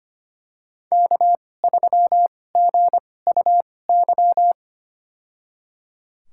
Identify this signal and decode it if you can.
Morse code, heard as a keyed tone, about 25 wpm, spaced as for 18 wpm, 700 Hz. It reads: K3GUY
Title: USA Call Signs - 2x Character Spacing - 25wpm